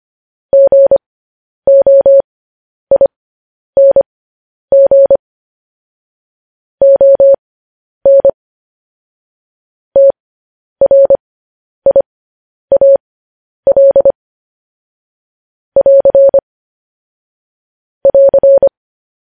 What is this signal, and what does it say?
Morse code, heard as a keyed tone, about 25 wpm, spaced as for 10 wpm, 550 Hz. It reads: GOING ON TRIAL <AR> <AR>